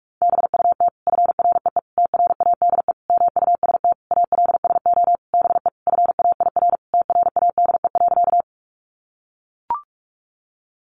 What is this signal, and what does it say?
Morse code, heard as a keyed tone, about 44 wpm, 700 Hz. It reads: BUT FREE TRADE MUST ALSO BE FAIR TRADE.
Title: Sentences from Top 1300 Words 45wpm